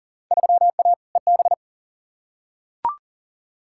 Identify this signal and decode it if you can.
Morse code, heard as a keyed tone, about 40 wpm, 700 Hz. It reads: 3A EB
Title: Field Day Exchanges Encore - 40wpm